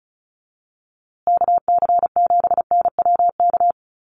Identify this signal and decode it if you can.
Morse code, heard as a keyed tone, about 34 wpm, 700 Hz. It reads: KC7NWK
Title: USA Call Signs Repeated 3x 35wpm